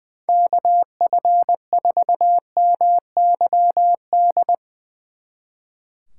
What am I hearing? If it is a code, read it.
Morse code, heard as a keyed tone, about 20 wpm, 700 Hz. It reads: KF4MYD
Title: Speed Racing - USA 2x3 Call Signs 20wpm